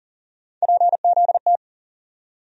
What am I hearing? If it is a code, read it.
Morse code, heard as a keyed tone, about 40 wpm, 700 Hz. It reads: PZT
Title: USA Call Sign Postfixes - 3 Letters 40wpm